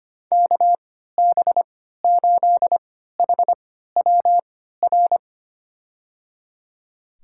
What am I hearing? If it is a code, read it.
Morse code, heard as a keyed tone, about 25 wpm, spaced as for 14 wpm, 700 Hz. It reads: KB8HWR
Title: USA Call Signs - 6 Characters - 3x Character Spacing - 25wpm